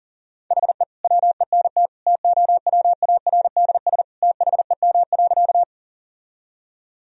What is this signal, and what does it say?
Morse code, heard as a keyed tone, about 40 wpm, 700 Hz. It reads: HE WENT TOWARDS THEM.
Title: Sentences from Top 1000 Words - Review Part 2 - 40wpm